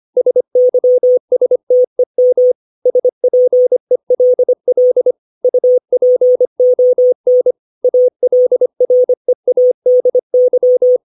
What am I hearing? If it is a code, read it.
Morse code, heard as a keyed tone, about 25 wpm, 500 Hz. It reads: SYSTEM SPELL UPON ALREADY
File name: 632 words / 632 Words 25WPM